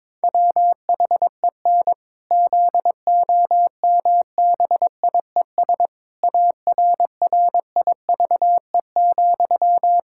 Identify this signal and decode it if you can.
Morse code, heard as a keyed tone, about 22 wpm, 700 Hz. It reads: WHEN ZOMBIES ARRIVE,
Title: Pangram Challenge v2 - 22wpm